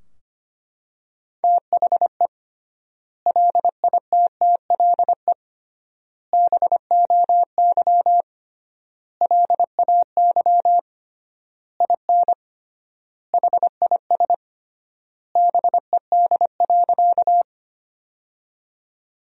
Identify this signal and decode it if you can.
Morse code, heard as a keyed tone, about 25 wpm, 700 Hz. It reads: THE LITTLE BOY LAY IN HIS BED.
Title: 3x Word-Spacing - Sentences from Top 500 Words - 25wpm